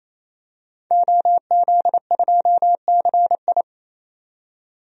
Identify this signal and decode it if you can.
Morse code, heard as a keyed tone, about 28 wpm, 700 Hz. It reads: OZ2CI
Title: DXCC-100 Call Signs - 5 Characters - 28wpm